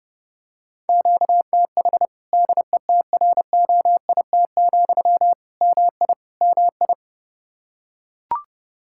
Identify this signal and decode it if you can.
Morse code, heard as a keyed tone, about 32 wpm, 700 Hz. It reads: QTH DETROIT, MI MI
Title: Practice QSO - Exchange 1 - Beginning - 30wpm